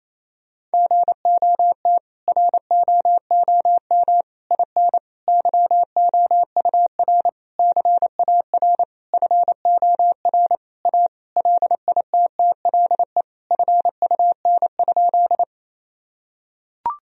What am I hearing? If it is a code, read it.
Morse code, heard as a keyed tone, about 28 wpm, 700 Hz. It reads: GOT ROOM IN YOUR CAR FOR A LITTLE FUN?
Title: Sentences from Top 1000 Words - Review Part 1 - 28wpm